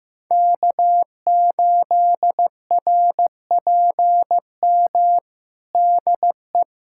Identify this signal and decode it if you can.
Morse code, heard as a keyed tone, about 15 wpm, 700 Hz. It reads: K8RPM DE
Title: USA Callsign DE Callsign - 5 Characters - 15wpm